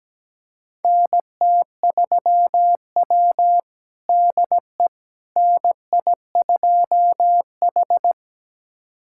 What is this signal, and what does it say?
Morse code, heard as a keyed tone, about 17 wpm, 700 Hz. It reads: NT3W DE NI2H
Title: USA Callsign DE Callsign - 4 Characters - 17wpm